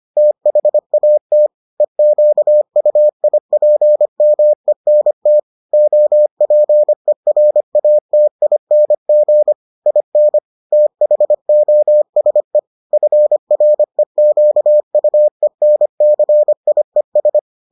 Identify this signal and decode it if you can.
Morse code, heard as a keyed tone, about 25 wpm, 600 Hz. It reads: THAT EQUIPMENT OPERATING IN THOSE FREQUENCIES